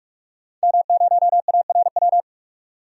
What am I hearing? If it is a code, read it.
Morse code, heard as a keyed tone, about 46 wpm, 700 Hz. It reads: M0ARW
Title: DXCC-100 Call Signs - 5 Characters - 45wpm